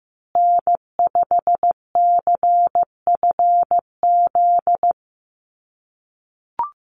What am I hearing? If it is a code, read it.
Morse code, heard as a keyed tone, about 15 wpm, 700 Hz. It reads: N5CFZ
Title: USA Call Signs Repeated 2x 15wpm